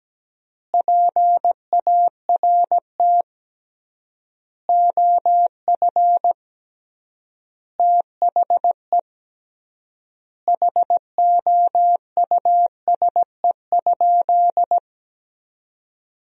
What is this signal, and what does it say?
Morse code, heard as a keyed tone, about 17 wpm, 700 Hz. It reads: PART OF THE HOUSE?
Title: 3x Word-Spacing - Sentences from Top 200 Words - 17wpm